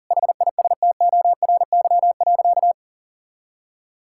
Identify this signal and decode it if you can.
Morse code, heard as a keyed tone, about 40 wpm, 700 Hz. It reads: HISTORY.
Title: Sentences from Top 1000 Words - Review Part 5 - 40wpm